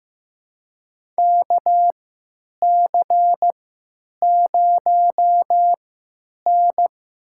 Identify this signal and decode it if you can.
Morse code, heard as a keyed tone, about 15 wpm, spaced as for 9 wpm, 700 Hz. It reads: KC0N
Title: USA Call Signs - 3x Character Spacing - 15wpm